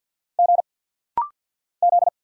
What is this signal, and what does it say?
Morse code, heard as a keyed tone, about 47 wpm, 690 Hz. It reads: G Z
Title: Single Letter-Number Alt - Rapid-Fire 50wpm